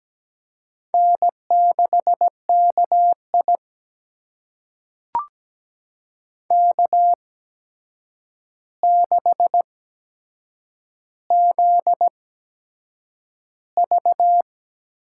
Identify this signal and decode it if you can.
Morse code, heard as a keyed tone, about 17 wpm, spaced as for 7 wpm, 700 Hz. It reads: N6KI K6ZV
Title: USA Call Signs - 4 Characters - 8x Character Spacing - 17wpm